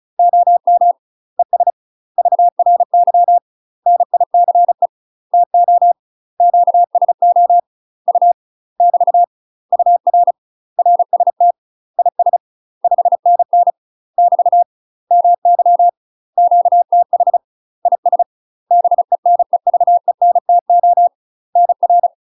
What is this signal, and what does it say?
Morse code, heard as a keyed tone, about 35 wpm, 700 Hz. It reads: OM ES VRY NICE TO QSO U <BT> UR RST IS 5NN <BT> MY QTH IS BENEVENTO NR
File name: UK MASTER QSOs TRAINING / 35WPM / UK MASTER QSOs 0013